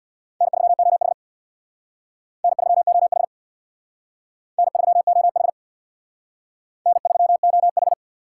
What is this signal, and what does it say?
Morse code, heard as a keyed tone, about 64 wpm, 700 Hz. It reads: N3OH N3OH N3OH N3OH